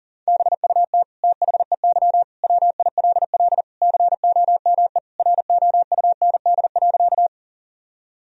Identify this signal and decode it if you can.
Morse code, heard as a keyed tone, about 40 wpm, 700 Hz. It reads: BUT THEY WILL COME ROUND.
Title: Sentences from Top 1000 Words - Review Part 1 - 40wpm